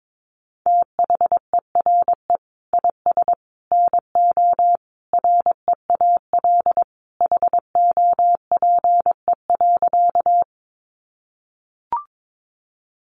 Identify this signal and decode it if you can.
Morse code, heard as a keyed tone, about 22 wpm, 700 Hz. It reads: THERE IS NO REAL HOPE.
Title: Sentences from Top 1000 Words 22wpm